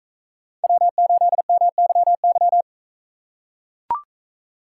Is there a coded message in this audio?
Yes: Morse code from 0.6 to 2.6 s, heard as a keyed tone, about 43 wpm, 700 Hz. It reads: W8MYY